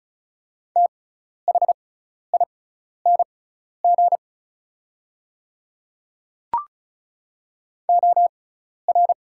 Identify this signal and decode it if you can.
Morse code, heard as a keyed tone, about 36 wpm, spaced as for 12 wpm, 700 Hz. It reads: THING OR
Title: 6x Character Spacing - Top 100 Words - 35wpm